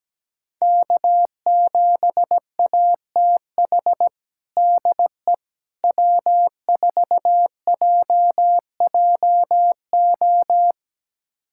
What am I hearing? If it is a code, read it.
Morse code, heard as a keyed tone, about 17 wpm, 700 Hz. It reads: K7ATH DE W4JJO
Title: USA Callsign DE Callsign - 5 Characters - 17wpm